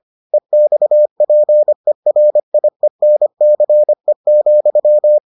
Morse code, heard as a keyed tone, about 25 wpm, 600 Hz. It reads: EXPERIENCE,